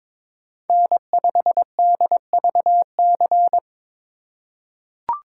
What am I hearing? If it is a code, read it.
Morse code, heard as a keyed tone, about 22 wpm, 700 Hz. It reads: N5DVC
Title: USA Call Signs - 5 Characters - 6x Character Spacing - 22wpm